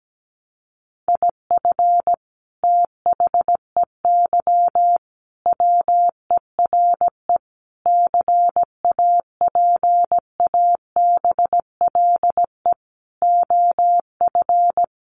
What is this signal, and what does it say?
Morse code, heard as a keyed tone, about 17 wpm, 700 Hz. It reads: IF THEY WERE CAPABLE OF
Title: Sentences from Top 1200 Words 17wpm